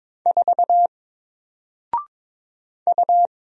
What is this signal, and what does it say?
Morse code, heard as a keyed tone, about 22 wpm, 700 Hz. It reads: 4 U
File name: Single-Letter-Number-22wpm